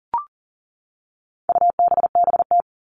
Morse code, heard as a keyed tone, about 40 wpm, 700 Hz. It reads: UBBT
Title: All Cut Numbers - 4-Digit 40wpm